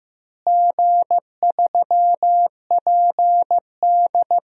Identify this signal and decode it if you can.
Morse code, heard as a keyed tone, about 16 wpm, 700 Hz. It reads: G3PD
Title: DXCC-100 Call Signs - 4 Characters - Repeated 2x - 15wpm